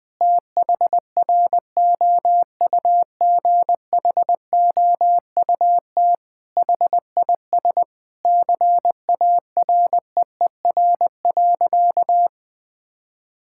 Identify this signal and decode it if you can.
Morse code, heard as a keyed tone, about 20 wpm, 700 Hz. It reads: THROUGHOUT HIS CAREER.
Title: Sentences from Top 1000 Words - Review Part 6 - 20wpm